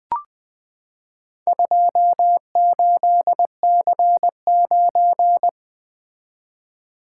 Morse code, heard as a keyed tone, about 20 wpm, 700 Hz. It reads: 28C9
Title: Four Letter-Number Combinations - 20wpm